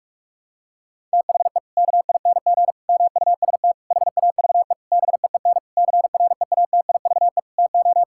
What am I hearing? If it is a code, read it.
Morse code, heard as a keyed tone, about 47 wpm, 700 Hz. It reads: THE KING MUST HAVE BEEN CREATIVE TO
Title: Sentences from Top 1000 Words - Review Part 5 - 45wpm